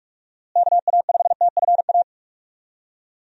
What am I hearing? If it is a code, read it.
Morse code, heard as a keyed tone, about 46 wpm, 700 Hz. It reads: KA5TFA
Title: USA Call Signs - 6 Characters - Repeated 2x - 45wpm